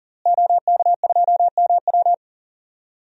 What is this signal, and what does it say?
Morse code, heard as a keyed tone, about 37 wpm, 700 Hz. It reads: OK2MW
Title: DXCC-100 Call Signs - 5 Characters - Repeated 3x - 40wpm